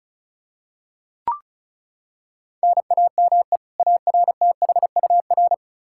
Morse code, heard as a keyed tone, about 36 wpm, 700 Hz. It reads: NAME ARTHUR
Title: Practice QSO - Exchange 1 - Beginning - 35wpm